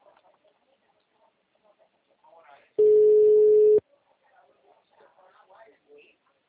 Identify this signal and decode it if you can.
Ringback tone (Europe, 425 Hz), with speech.